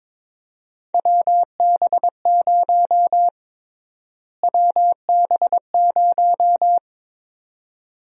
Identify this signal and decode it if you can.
Morse code, heard as a keyed tone, about 21 wpm, 700 Hz. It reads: WB0 WB0
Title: USA Call Sign Prefixes - Repeated 2x - 22wpm